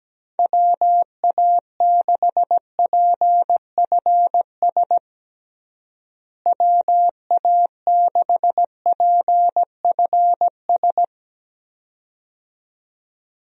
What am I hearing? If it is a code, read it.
Morse code, heard as a keyed tone, about 17 wpm, 700 Hz. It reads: WA6PFS WA6PFS
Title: USA Call Signs - 6 Characters - Repeated 2x - 17wpm